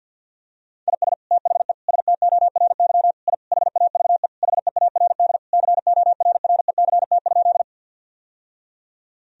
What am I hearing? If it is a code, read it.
Morse code, heard as a keyed tone, about 48 wpm, 700 Hz. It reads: IS THE STORY I HAVE HEARD CORRECT?
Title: Sentences from Top 1000 Words - Review Part 2 - 50wpm